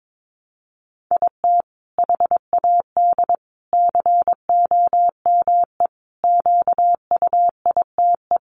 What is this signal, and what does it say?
Morse code, heard as a keyed tone, about 22 wpm, 700 Hz. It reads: IT HAD COME QUITE
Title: Sentences from Top 1300 Words 22wpm